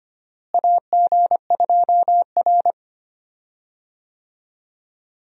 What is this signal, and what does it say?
Morse code, heard as a keyed tone, about 25 wpm, 700 Hz. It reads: AG2R
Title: USA Call Signs - 4 Characters - Repeated 3x - 25wpm